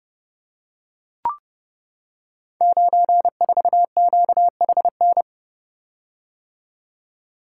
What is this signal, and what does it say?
Morse code, heard as a keyed tone, about 30 wpm, 700 Hz. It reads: 94QHN
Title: Code Groups - 5 Letter:Numbers - 30wpm